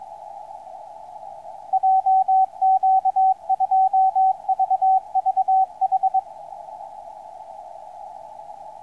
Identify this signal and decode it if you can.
Morse code, heard as a keyed tone, about 24 wpm, 760 Hz. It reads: JQ2VVH
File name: oe3xma 22-10-13 1958 160m